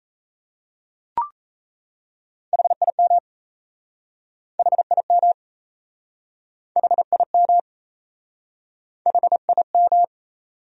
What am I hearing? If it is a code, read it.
Morse code, heard as a keyed tone, about 38 wpm, 700 Hz. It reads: HIM HIM HIM HIM